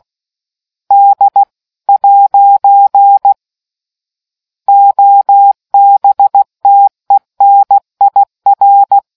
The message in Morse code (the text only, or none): D ' OBTENIR